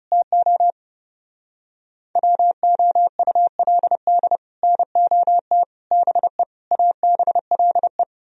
Morse code, heard as a keyed tone, about 39 wpm, 700 Hz. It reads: TO WOULD NOT BE ABLE